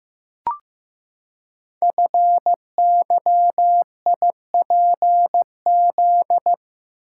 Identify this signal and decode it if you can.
Morse code, heard as a keyed tone, about 15 wpm, 700 Hz. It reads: FYIPZ
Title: Code Groups - 5 Letter:Numbers - 15wpm